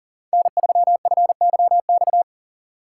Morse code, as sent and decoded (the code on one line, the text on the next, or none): -. ...-- ..-. -.-- -..-
N3FYX